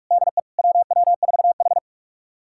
Morse code, heard as a keyed tone, about 42 wpm, 700 Hz. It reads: DE WW4H
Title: USA Callsign DE Callsign - 4 Characters - 45wpm